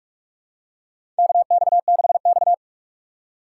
Morse code, heard as a keyed tone, about 46 wpm, 700 Hz. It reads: KX6X
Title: K1USN SST QSO Practice - 45wpm